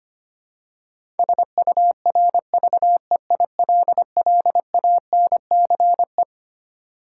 Morse code, heard as a keyed tone, about 25 wpm, 700 Hz. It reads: SURVEILLANCE
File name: Twelve Letter Words - 25wpm